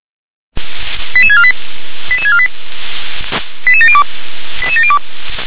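Five-tone selective call (ZVEI), heard as quick runs of five tones, tones of about 70 ms. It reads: 88648 88648 88702 88702
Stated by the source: ZVEI five-tone selective call received by SDR.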